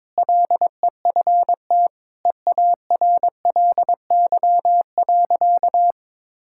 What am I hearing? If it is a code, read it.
Morse code, heard as a keyed tone, about 22 wpm, 700 Hz. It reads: LEFT EARLY.
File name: practice / 22wpm / Sentences with 3-Word Phrases - 22wpm